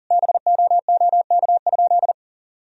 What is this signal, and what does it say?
Morse code, heard as a keyed tone, about 40 wpm, 700 Hz. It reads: BOOK?
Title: Sentences from Top 1000 Words - Review Part 2 - 40wpm